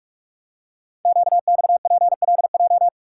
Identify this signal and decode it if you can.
Morse code, heard as a keyed tone, about 44 wpm, 700 Hz. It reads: QXPLJ
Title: Code Groups - 5 Letters - 45wpm